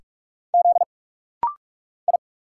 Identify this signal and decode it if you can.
Morse code, heard as a keyed tone, about 44 wpm, 700 Hz. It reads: Z I
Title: Single Letters - Rapid-Fire 45wpm